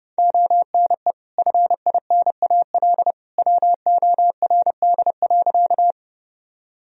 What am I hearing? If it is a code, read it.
Morse code, heard as a keyed tone, about 32 wpm, 700 Hz. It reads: ONE FINAL WORD.
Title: Sentences from Top 1000 Words - Review Part 2 - 30wpm